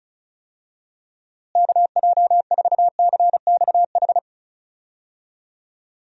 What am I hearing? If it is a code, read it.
Morse code, heard as a keyed tone, about 36 wpm, 700 Hz. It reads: KJ4CXH
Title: USA Call Signs - 6 Characters - Repeated 3x - 35wpm